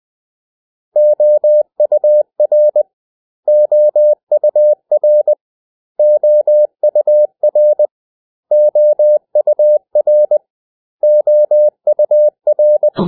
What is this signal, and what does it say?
Morse code, heard as a keyed tone, about 20 wpm, 600 Hz. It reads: OUR OUR OUR OUR OUR
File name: cw / 100 Common English Words